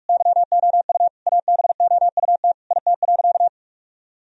Morse code, heard as a keyed tone, about 42 wpm, 700 Hz. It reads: YOU ABOUT IT.
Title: Sentences from Top 1000 Words - Review Part 1 - 45wpm